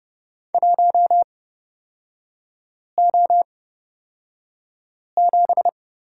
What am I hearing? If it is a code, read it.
Morse code, heard as a keyed tone, about 30 wpm, spaced as for 5 wpm, 700 Hz. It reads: 1O7
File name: Lesson 36e - BT - Three Characters 30wpm